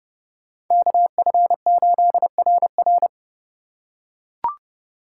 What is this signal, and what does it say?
Morse code, heard as a keyed tone, about 28 wpm, 700 Hz. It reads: KF8RR